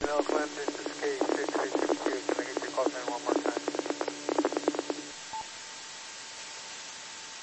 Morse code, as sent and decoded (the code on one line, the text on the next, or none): none